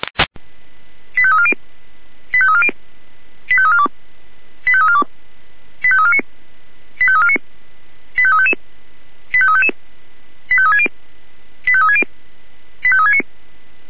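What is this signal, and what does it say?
Five-tone selective call (ZVEI), heard as quick runs of five tones, tones of about 70 ms. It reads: 85340 85340 85342 85342 85349 85349 85355 85355 85360 85360 85369
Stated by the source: ZVEI five-tone call received on SDR.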